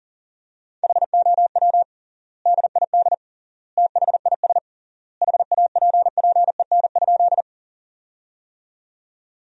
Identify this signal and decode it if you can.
Morse code, heard as a keyed tone, about 40 wpm, 700 Hz. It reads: HOW DID THIS HAPPEN?
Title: 3x Word-Spacing - Sentences from Top 400 Words - 40wpm